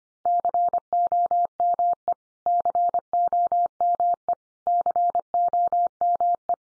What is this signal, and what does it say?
Morse code, heard as a keyed tone, about 25 wpm, 700 Hz. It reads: COME COME COME
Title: Top 100 Words 25wpm x3